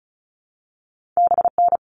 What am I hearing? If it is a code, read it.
Morse code, heard as a keyed tone, about 36 wpm, 690 Hz. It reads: BN
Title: All Cut Numbers - 2-Digit 35wpm